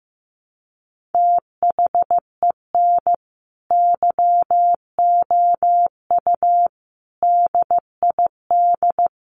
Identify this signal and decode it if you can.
Morse code, heard as a keyed tone, about 15 wpm, 700 Hz. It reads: THEN YOU DID
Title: Sentences from Top 600 Words 15wpm